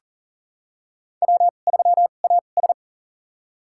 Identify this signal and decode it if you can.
Morse code, heard as a keyed tone, about 40 wpm, spaced as for 29 wpm, 700 Hz. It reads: W3AS
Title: USA Call Signs - 4 Characters - 2x Character Spacing - 40wpm